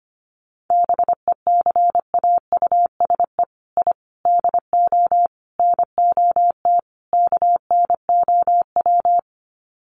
Morse code, heard as a keyed tone, about 25 wpm, 700 Hz. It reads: BECAUSE I DO NOT KNOW
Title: Sets of 5 Words Encore2 25wpm